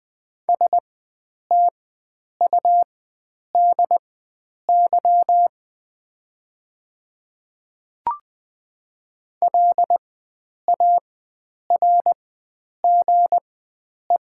Morse code, heard as a keyed tone, about 21 wpm, spaced as for 9 wpm, 700 Hz. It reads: STUDY LARGE
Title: 4x Character Spacing - Top 500 Words - 20wpm